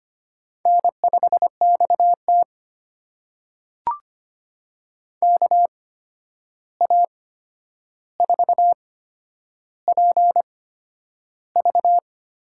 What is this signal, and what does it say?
Morse code, heard as a keyed tone, about 25 wpm, spaced as for 10 wpm, 700 Hz. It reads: N5XT KA4PV
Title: USA Call Signs - 8x Character Spacing - 25wpm